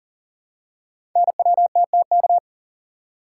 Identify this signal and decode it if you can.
Morse code, heard as a keyed tone, about 38 wpm, 690 Hz. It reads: NWTTK
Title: Code Groups - 5 Letters - 40wpm